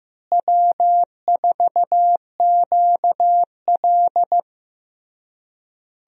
Morse code, heard as a keyed tone, about 15 wpm, 700 Hz. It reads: W4QL
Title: USA Call Signs - 4 Characters - Repeated 3x - 15wpm